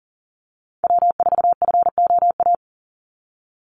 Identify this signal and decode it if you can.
Morse code, heard as a keyed tone, about 40 wpm, 700 Hz. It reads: W4FOA
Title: Call Signs - 40wpm